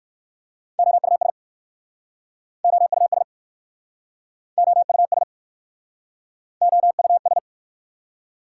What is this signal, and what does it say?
Morse code, heard as a keyed tone, about 60 wpm, 700 Hz. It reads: OUS OUS OUS OUS